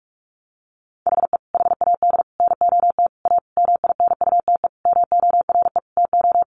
Morse code, heard as a keyed tone, about 44 wpm, 700 Hz. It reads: HE HAD NOT A MINUTE MORE TO
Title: Sentences from Top 1100 Words 45wpm